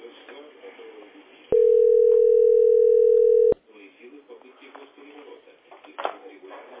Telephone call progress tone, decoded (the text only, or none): ringback tone